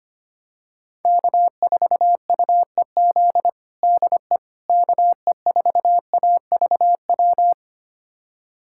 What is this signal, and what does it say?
Morse code, heard as a keyed tone, about 25 wpm, 700 Hz. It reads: K4UEZ DE KE4AVW
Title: USA Callsign DE Callsign 25wpm